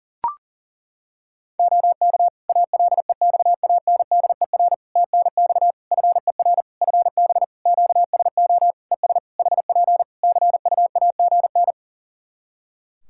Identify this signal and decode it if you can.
Morse code, heard as a keyed tone, about 40 wpm, 700 Hz. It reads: OK ALEXANDER TNX FER FB QSO ES HP CUAGN
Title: K1USN SST QSO Practice 40wpm